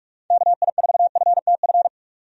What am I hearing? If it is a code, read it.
Morse code, heard as a keyed tone, about 46 wpm, 700 Hz. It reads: KI4FTF
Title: USA Call Signs - 6 Characters - 45wpm